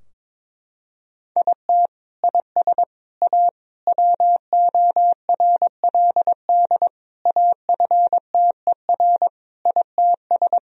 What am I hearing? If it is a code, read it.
Morse code, heard as a keyed tone, about 22 wpm, 700 Hz. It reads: IT IS A WORLD AFTER ITS
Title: Sentences from Top 1000 Words - Review Part 4 - 22wpm